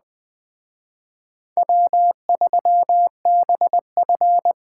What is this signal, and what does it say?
Morse code, heard as a keyed tone, about 20 wpm, 700 Hz. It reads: W3BF
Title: Speed Racing - USA 1x2 Call Signs 17wpm